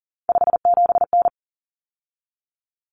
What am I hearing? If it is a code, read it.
Morse code, heard as a keyed tone, about 40 wpm, 700 Hz. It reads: 57N
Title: Common Cut Numbers - 3-Digit 40wpm